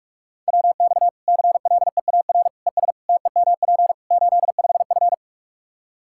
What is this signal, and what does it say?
Morse code, heard as a keyed tone, about 46 wpm, 700 Hz. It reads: WX CLEAR ES TEMP 85F
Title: Practice QSO - Exchange 2 - Info - 45wpm